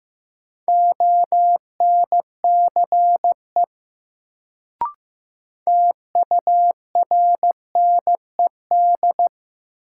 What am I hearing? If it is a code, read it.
Morse code, heard as a keyed tone, about 15 wpm, 700 Hz. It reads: ONCE TURNED
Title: Sets of 4 Words - New Word Drill - 15wpm